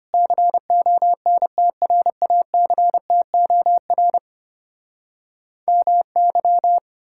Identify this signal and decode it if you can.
Morse code, heard as a keyed tone, about 33 wpm, 700 Hz. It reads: CONTRACTOR MY